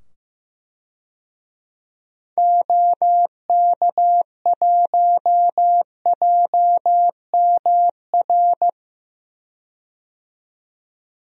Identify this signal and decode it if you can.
Morse code, heard as a keyed tone, about 15 wpm, 700 Hz. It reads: OK1JMR
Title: DXCC-100 Call Signs - 6 Characters - 15wpm